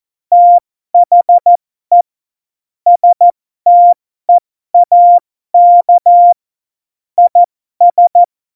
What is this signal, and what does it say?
Morse code, heard as a keyed tone, about 14 wpm, spaced as for 12 wpm, 700 Hz. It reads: THE STEAK IS
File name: Morsekurs / Witze